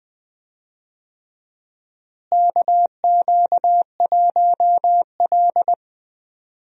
Morse code, heard as a keyed tone, about 20 wpm, 700 Hz. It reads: KQ1L
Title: USA Call Signs - 4 Characters - Repeated 3x - 20wpm